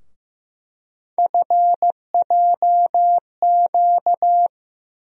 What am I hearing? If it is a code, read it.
Morse code, heard as a keyed tone, about 15 wpm, 700 Hz. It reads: FJQ
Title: USA Call Sign Postfixes - 3 Letters 15wpm